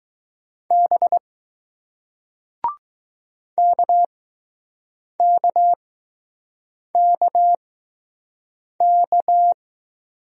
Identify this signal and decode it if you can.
Morse code, heard as a keyed tone, about 20 wpm, spaced as for 6 wpm, 700 Hz. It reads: B KKKK